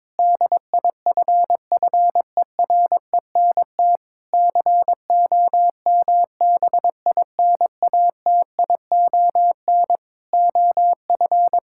Morse code, heard as a keyed tone, about 22 wpm, 700 Hz. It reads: DIFFERENT COMBINATION OF
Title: Sentences from Top 1000 Words - Review Part 6 - 22wpm